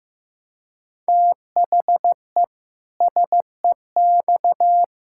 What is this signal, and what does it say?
Morse code, heard as a keyed tone, about 15 wpm, 700 Hz. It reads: THE SEX
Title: Pangram Challenge v2 - 15wpm